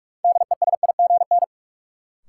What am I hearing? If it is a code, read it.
Morse code, heard as a keyed tone, about 42 wpm, 700 Hz. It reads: DESIGN
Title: Sentences from Top 800 Words - New Word Drill - 45wpm